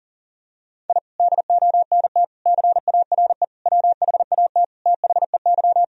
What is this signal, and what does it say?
Morse code, heard as a keyed tone, about 40 wpm, 700 Hz. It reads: I DONT CARE WHAT THEY
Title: Sentences from Top 1000 Words - Review Part 1 - 40wpm